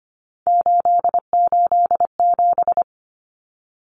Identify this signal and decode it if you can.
Morse code, heard as a keyed tone, about 25 wpm, 700 Hz. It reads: 887
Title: Common Cut Numbers - 3-Digit 25wpm